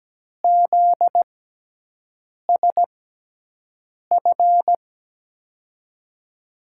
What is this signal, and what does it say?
Morse code, heard as a keyed tone, about 17 wpm, spaced as for 6 wpm, 700 Hz. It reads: ZSF
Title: USA Call Signs - 6 Characters - 6x Character Spacing - 17wpm